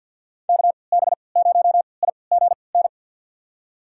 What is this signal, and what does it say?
Morse code, heard as a keyed tone, about 53 wpm, spaced as for 29 wpm, 700 Hz. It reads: KB0IGN
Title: USA Call Signs - 3x Character Spacing - 50wpm